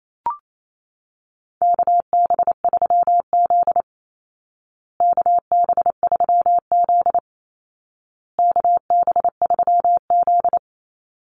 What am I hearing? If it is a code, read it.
Morse code, heard as a keyed tone, about 28 wpm, 700 Hz. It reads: KB3Z KB3Z KB3Z
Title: Call Signs Repeated 2x - 28wpm